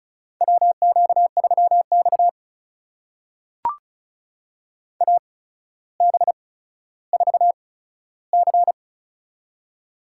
Morse code, heard as a keyed tone, about 36 wpm, spaced as for 15 wpm, 700 Hz. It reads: WQ3X AB4C
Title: USA Call Signs - 8x Character Spacing - 35wpm